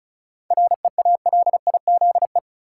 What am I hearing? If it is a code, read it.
Morse code, heard as a keyed tone, about 34 wpm, 700 Hz. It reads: REALIZE
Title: Seven Letter Words - 35wpm